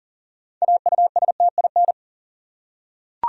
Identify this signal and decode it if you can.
Morse code, heard as a keyed tone, about 40 wpm, 700 Hz. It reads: AUSTIN
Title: USA Common Names - 40wpm